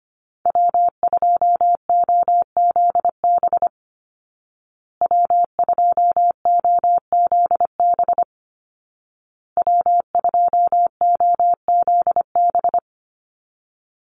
Morse code, heard as a keyed tone, about 25 wpm, 700 Hz. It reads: W2OZB W2OZB W2OZB
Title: Call Signs Repeated 2x - 25wpm